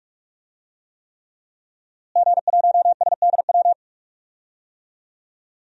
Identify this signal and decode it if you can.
Morse code, heard as a keyed tone, about 44 wpm, 700 Hz. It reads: G1SDW
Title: DXCC-100 Call Signs - 5 Characters - Repeated 3x - 45wpm